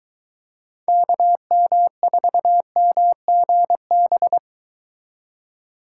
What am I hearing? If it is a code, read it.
Morse code, heard as a keyed tone, about 22 wpm, 700 Hz. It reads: KM4MGB